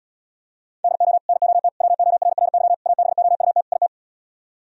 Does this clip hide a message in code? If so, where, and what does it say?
Yes: Morse code from 0.8 to 3.9 s, heard as a keyed tone, about 78 wpm, 700 Hz. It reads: DO NOT WORRY ABOUT IT